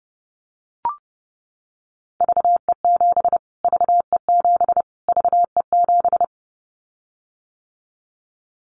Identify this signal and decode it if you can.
Morse code, heard as a keyed tone, about 30 wpm, 700 Hz. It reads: VE7 VE7 VE7
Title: Callsign Prefixes for Top 25 DXCC - Repeated 3x - 30wpm